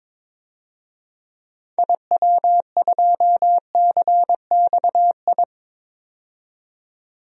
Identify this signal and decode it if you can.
Morse code, heard as a keyed tone, about 22 wpm, 700 Hz. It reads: IW2CXI
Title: DXCC-100 Call Signs - 6 Characters - 22wpm